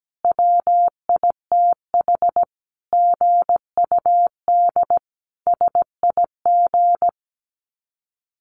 Morse code, heard as a keyed tone, about 17 wpm, 700 Hz. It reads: WITH GUD SIG
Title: Common QSO Phrases 17wpm